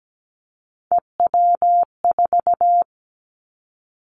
Morse code, heard as a keyed tone, about 17 wpm, 700 Hz. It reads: EW4
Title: Callsign Prefixes for Top 50 DXCC - 17wpm